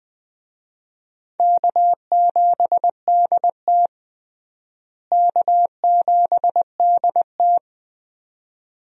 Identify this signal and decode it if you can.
Morse code, heard as a keyed tone, about 21 wpm, 700 Hz. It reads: K7DT K7DT
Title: USA Call Signs - 4 Characters - Repeated 2x - 20wpm